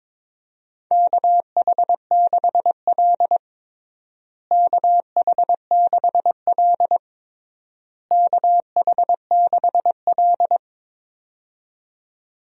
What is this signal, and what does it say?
Morse code, heard as a keyed tone, about 22 wpm, 700 Hz. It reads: KH6L KH6L KH6L
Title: DXCC-100 Call Signs - 4 Characters - Repeated 3x - 22wpm